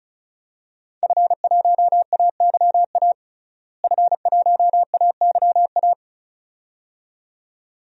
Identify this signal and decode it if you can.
Morse code, heard as a keyed tone, about 36 wpm, 700 Hz. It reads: F1AYA F1AYA
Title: DXCC-100 Call Signs - 5 Characters - Repeated 2x - 35wpm